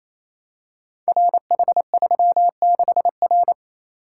Morse code, heard as a keyed tone, about 27 wpm, 700 Hz. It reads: RH36R
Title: Code Groups - 5 Letter:Numbers - 28wpm